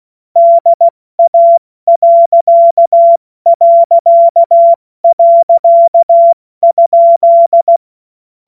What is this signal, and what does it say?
Morse code, heard as a keyed tone, about 16 wpm, spaced as for 14 wpm, 680 Hz. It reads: DA...?
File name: Morsekurs / Witze / Witz 17 Juni Mai 2025 70 F80